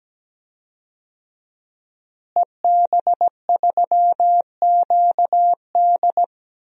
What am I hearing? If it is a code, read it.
Morse code, heard as a keyed tone, about 17 wpm, 700 Hz. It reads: EB3QD
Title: DXCC-100 Call Signs - 5 Characters - Repeated 2x - 17wpm